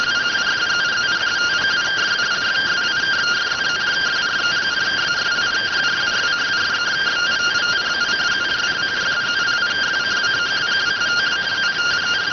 Two-tone teleprinter signal, about 170 Hz shift.